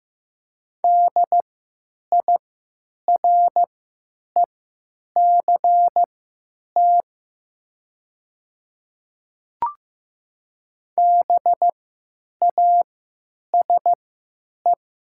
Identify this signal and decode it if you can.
Morse code, heard as a keyed tone, about 15 wpm, spaced as for 9 wpm, 700 Hz. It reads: DIRECT BASE
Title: 3x Character Spacing - Top 500 Words - 15wpm